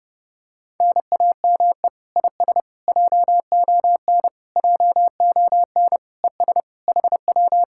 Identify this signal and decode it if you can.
Morse code, heard as a keyed tone, about 30 wpm, 700 Hz. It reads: NAME IS JON JON ES HW